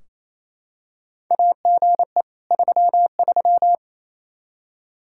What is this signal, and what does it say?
Morse code, heard as a keyed tone, about 27 wpm, 700 Hz. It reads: AGE 33
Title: K1USN SST QSO Practice 28wpm